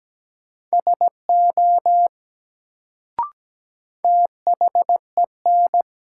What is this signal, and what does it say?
Morse code, heard as a keyed tone, about 17 wpm, 700 Hz. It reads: SO THEN
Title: Sets of 2 Words - New Word Drill - 17wpm